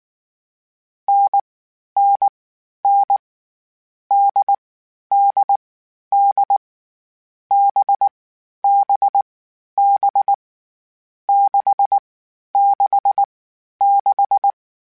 Morse code, heard as a keyed tone, about 19 wpm, 800 Hz. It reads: N N N D D D B B B 6 6 6
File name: Difficult Combos / Difficult Combos-mp3s / NDB NDB6 @18